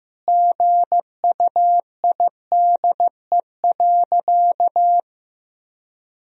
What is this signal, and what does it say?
Morse code, heard as a keyed tone, about 15 wpm, 700 Hz. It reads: GUIDE.
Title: Sentences from Top 1000 Words - Review Part 3 - 15wpm